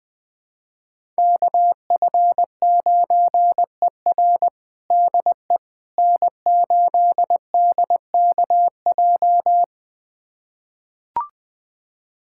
Morse code, heard as a keyed tone, about 20 wpm, 700 Hz. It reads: KF9ER DE N8DKJ
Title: USA Callsign DE Callsign 20wpm